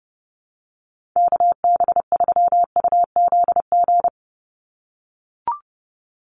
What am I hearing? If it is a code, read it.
Morse code, heard as a keyed tone, about 28 wpm, 700 Hz. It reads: KB3UZG
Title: USA Call Signs 30wpm